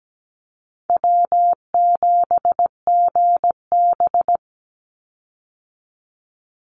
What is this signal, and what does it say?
Morse code, heard as a keyed tone, about 17 wpm, 700 Hz. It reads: W7GB
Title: Easy Call Signs - 17wpm